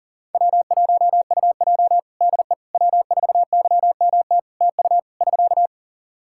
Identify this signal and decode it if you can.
Morse code, heard as a keyed tone, about 40 wpm, 700 Hz. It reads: W1UJ DE W4YMT TU <SK>
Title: Practice QSO - Exchange 4 - Ending - 40wpm